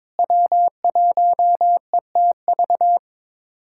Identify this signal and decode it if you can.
Morse code, heard as a keyed tone, about 22 wpm, 700 Hz. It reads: W1ETV
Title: USA Callsign DE Callsign - 5 Characters - 22wpm